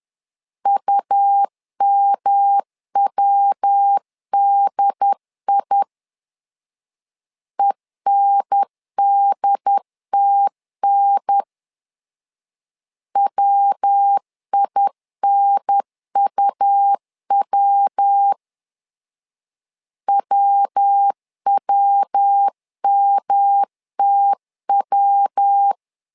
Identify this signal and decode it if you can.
Morse code, heard as a keyed tone, about 10 wpm, 800 Hz. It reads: UMWDI ENDTN WINUW WWMTW
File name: LEZIONI / 3 MWRO / 3 MW 10wpm 1959